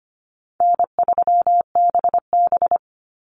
Morse code, heard as a keyed tone, about 25 wpm, 700 Hz. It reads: N3BB
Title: Call Signs Repeated 3x - 25wpm